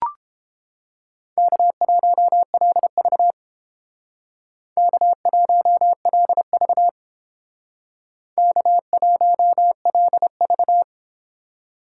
Morse code, heard as a keyed tone, about 30 wpm, 700 Hz. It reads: K1LV K1LV K1LV